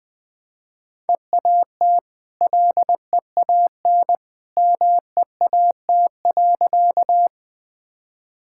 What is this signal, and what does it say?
Morse code, heard as a keyed tone, about 21 wpm, 700 Hz. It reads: EAT LEAN MEAT.
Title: Easy Sentences with Hints - 20wpm